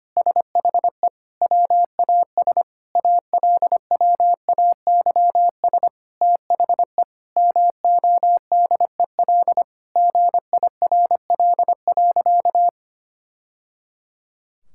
Morse code, heard as a keyed tone, about 25 wpm, 700 Hz. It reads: SHE WAS ALWAYS THE MODEL GIRL.
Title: Sentences from Top 1000 Words - Review Part 5 - 25wpm